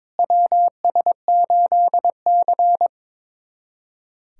Morse code, heard as a keyed tone, about 22 wpm, 700 Hz. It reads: WS8C
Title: USA Callsign DE Callsign - 4 Characters - 22wpm